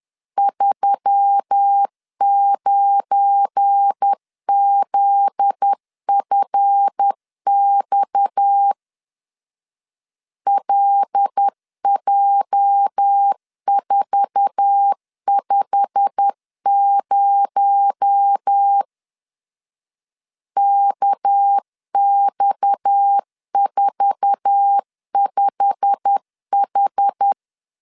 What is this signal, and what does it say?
Morse code, heard as a keyed tone, about 10 wpm, 800 Hz. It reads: 39ZFX LJ450 KX45H
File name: LEZIONI / 8 YJ45 / 8 YJ45 10wpm 2549